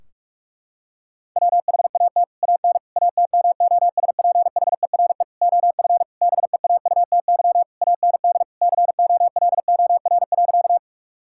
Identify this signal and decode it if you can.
Morse code, heard as a keyed tone, about 41 wpm, 700 Hz. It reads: WHAT AN ATMOSPHERE OF BEAUTY AND COLOR.
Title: Sentences from Top 1600 Words 45wpm